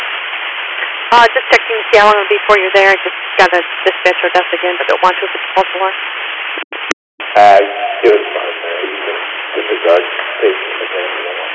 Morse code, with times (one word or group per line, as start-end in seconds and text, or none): none